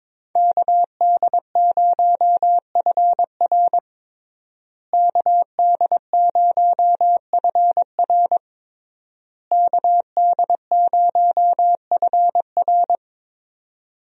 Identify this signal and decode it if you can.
Morse code, heard as a keyed tone, about 22 wpm, 700 Hz. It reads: KD0FR KD0FR KD0FR
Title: USA Call Signs - 5 Characters - Repeated 3x - 22wpm